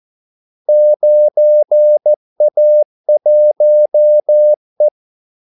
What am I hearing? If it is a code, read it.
Morse code, heard as a keyed tone, about 14 wpm, 600 Hz. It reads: 9A1E
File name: Callsigns-14wpm